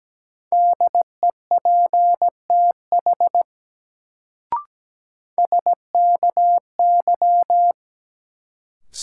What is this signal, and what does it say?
Morse code, heard as a keyed tone, about 17 wpm, 700 Hz. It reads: DEPTH SKY
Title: Sentences from Top 1200 Words - New Word Drill Alt - 17wpm